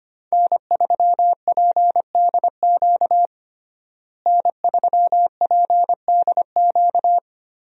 Morse code, heard as a keyed tone, about 25 wpm, 700 Hz. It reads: N3PDQ N3PDQ
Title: USA Call Signs - 5 Characters - Repeated 2x - 25wpm